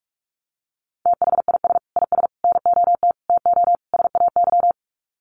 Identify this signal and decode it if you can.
Morse code, heard as a keyed tone, about 44 wpm, 700 Hz. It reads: THIS IS NOT TO SAY
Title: Sets of 5 Words Encore 45wpm